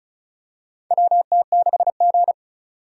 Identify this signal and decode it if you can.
Morse code, heard as a keyed tone, about 36 wpm, 700 Hz. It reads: WT6G
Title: USA Call Signs - 4 Characters - 3x Character Spacing - 35wpm